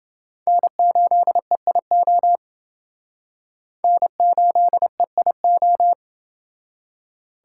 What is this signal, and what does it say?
Morse code, heard as a keyed tone, about 29 wpm, 700 Hz. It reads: N8EIO N8EIO